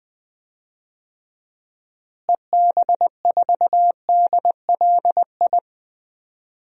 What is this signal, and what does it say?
Morse code, heard as a keyed tone, about 20 wpm, 700 Hz. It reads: EB4DLI
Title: DXCC-100 Call Signs - 6 Characters - Repeated 2x - 20wpm